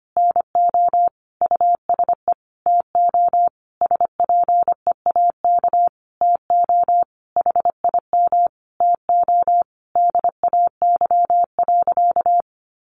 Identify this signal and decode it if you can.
Morse code, heard as a keyed tone, about 25 wpm, 700 Hz. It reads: NO USE TO SPEAK TO HIM TO DAY.
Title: Sentences from Top 1100 Words 25wpm